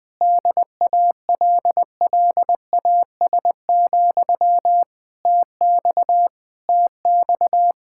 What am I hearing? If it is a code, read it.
Morse code, heard as a keyed tone, about 20 wpm, 700 Hz. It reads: DALLAS, TX TX
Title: Speed-Racing Common QSO Phrases Encore - 20wpm